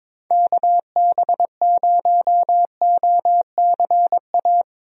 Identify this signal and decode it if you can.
Morse code, heard as a keyed tone, about 22 wpm, 700 Hz. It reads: KB0OCA
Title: USA Call Signs - 6 Characters - Repeated 2x - 22wpm